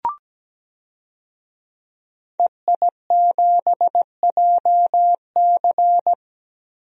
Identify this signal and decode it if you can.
Morse code, heard as a keyed tone, about 17 wpm, 700 Hz. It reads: EI7JC
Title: DXCC-100 Call Signs - 5 Characters - Repeated 2x - 17wpm